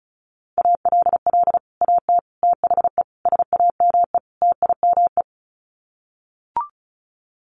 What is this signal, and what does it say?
Morse code, heard as a keyed tone, about 34 wpm, 700 Hz. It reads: ALL AT THE SAME TIME
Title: Sets of 5 Words Encore2 35wpm